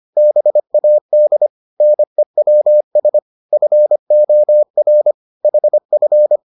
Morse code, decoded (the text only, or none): BAD NEWS FOR HF